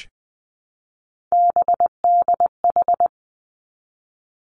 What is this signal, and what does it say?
Morse code, heard as a keyed tone, about 19 wpm, 700 Hz. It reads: BDH
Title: Three Challenging Letter-Number Combinations 20wpm